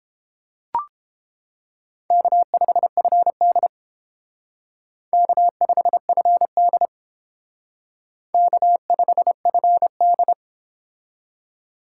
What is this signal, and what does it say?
Morse code, heard as a keyed tone, about 28 wpm, 700 Hz. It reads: K5FD K5FD K5FD